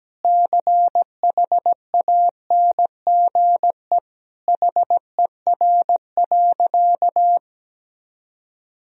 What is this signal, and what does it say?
Morse code, heard as a keyed tone, about 17 wpm, 700 Hz. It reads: CHANGE HER.
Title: Sentences from Top 1000 Words - Review Part 1 - 17wpm